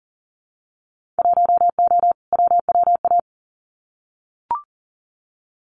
Morse code, heard as a keyed tone, about 40 wpm, 700 Hz. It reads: 1O WWA
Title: Winter Field Day Exchanges 40wpm